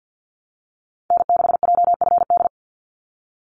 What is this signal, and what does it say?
Morse code, heard as a keyed tone, about 50 wpm, 700 Hz. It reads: N6JFD
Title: NA SOTA Activator Callsigns 50wpm